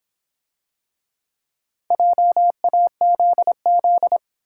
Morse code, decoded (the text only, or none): JAZZ